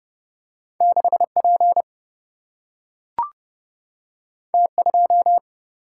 Morse code, heard as a keyed tone, about 30 wpm, 700 Hz. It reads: 6P T2
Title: Single-Letter Single-Number 30wpm